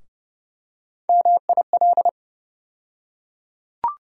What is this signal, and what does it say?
Morse code, heard as a keyed tone, about 30 wpm, 700 Hz. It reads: MIL
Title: USA Call Sign Postfixes - 3 Letters 30wpm